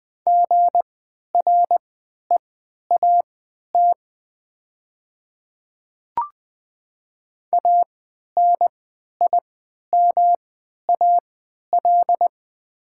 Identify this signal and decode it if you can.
Morse code, heard as a keyed tone, about 20 wpm, spaced as for 11 wpm, 700 Hz. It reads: GREAT ANIMAL
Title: 3x Character Spacing - Top 300 Words - 20wpm